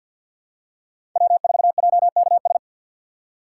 Morse code, heard as a keyed tone, about 50 wpm, 700 Hz. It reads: W4JKS
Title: USA Call Signs - 5 Characters - 2x Character Spacing - 50wpm